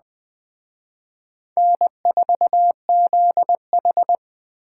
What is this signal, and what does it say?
Morse code, heard as a keyed tone, about 20 wpm, 700 Hz. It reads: N4ZH
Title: Speed Racing - USA Call Signs 17wpm